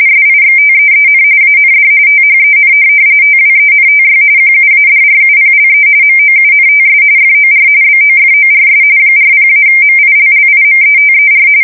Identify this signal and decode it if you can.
RTTY, about 45.45 baud. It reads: SIXTEEN AND THEN TWENTY QUADRUPLE MOUNTS FOR BOFORS 40-MILLIMETER
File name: fsk170 / USS-New-Jersey